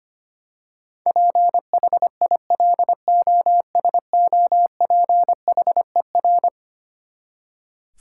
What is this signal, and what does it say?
Morse code, heard as a keyed tone, about 25 wpm, 700 Hz. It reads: PHILOSOPHER
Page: morse-code-ninja-practice-25wpm